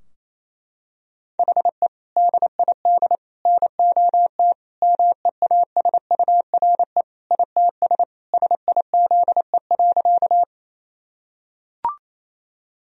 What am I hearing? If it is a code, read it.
Morse code, heard as a keyed tone, about 28 wpm, 700 Hz. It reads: HE DID NOT MEASURE ITS SIZE.
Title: Sentences from Top 1000 Words - Review Part 2 - 28wpm